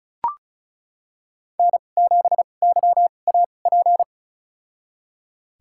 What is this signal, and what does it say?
Morse code, heard as a keyed tone, about 36 wpm, spaced as for 25 wpm, 700 Hz. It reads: N7YAP
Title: USA Call Signs - 5 Characters - 2x Character Spacing - 35wpm